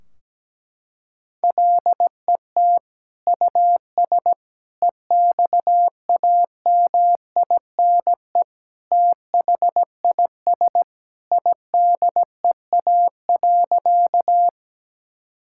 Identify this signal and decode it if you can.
Morse code, heard as a keyed tone, about 17 wpm, 700 Hz. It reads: LET US EXAMINE THIS IDEA.
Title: Sentences from Top 1600 Words 17wpm